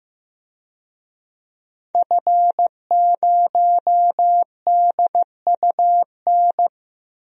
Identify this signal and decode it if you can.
Morse code, heard as a keyed tone, about 15 wpm, 700 Hz. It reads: F0DUN
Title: DXCC-100 Call Signs - 5 Characters - Repeated 2x - 15wpm